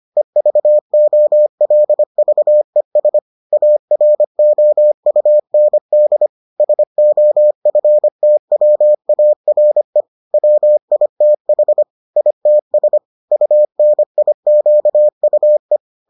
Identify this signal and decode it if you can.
Morse code, heard as a keyed tone, about 25 wpm, 600 Hz. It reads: EVOLVES AROUND SOFTWARE WITH ITS UNIQUE